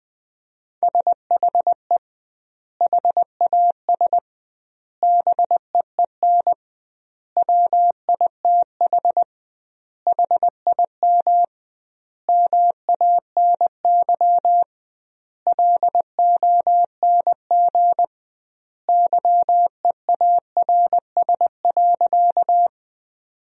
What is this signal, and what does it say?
Morse code, heard as a keyed tone, about 20 wpm, 700 Hz. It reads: SHE HAS BEEN WITH HIM MANY LONG YEARS.
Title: 2x Word-Spacing - Sentences from Top 200 Words - 20wpm